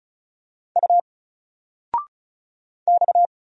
Morse code, heard as a keyed tone, about 35 wpm, 700 Hz. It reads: U X
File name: Single-Letter-Number-35wpm